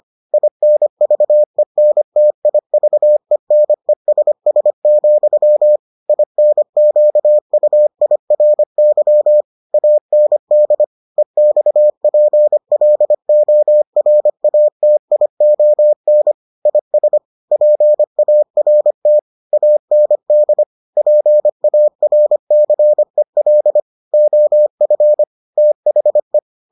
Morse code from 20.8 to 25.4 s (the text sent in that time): PARCEL OF